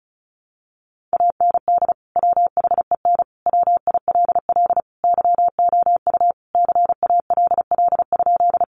Morse code, heard as a keyed tone, about 34 wpm, 700 Hz. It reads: AND WHEN WILL YOU CALL?
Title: Sentences from Top 100 Words 35wpm